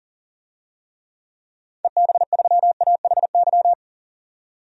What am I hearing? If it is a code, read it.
Morse code, heard as a keyed tone, about 40 wpm, 700 Hz. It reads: EB3AHY
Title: DXCC-100 Call Signs - 6 Characters - 40wpm